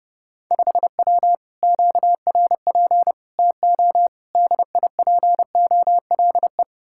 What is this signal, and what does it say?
Morse code, heard as a keyed tone, about 28 wpm, 700 Hz. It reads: 5W QRP TO DIPOLE